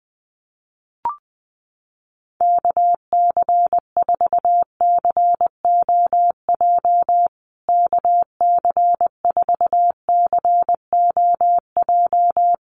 Morse code, heard as a keyed tone, about 20 wpm, 700 Hz. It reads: KC4COJ KC4COJ
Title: USA Call Signs Repeated 2x 20wpm